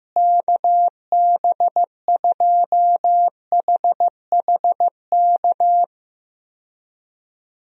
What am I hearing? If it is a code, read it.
Morse code, heard as a keyed tone, about 15 wpm, 700 Hz. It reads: KB2HHK
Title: USA Call Signs - 6 Characters - Repeated 2x - 15wpm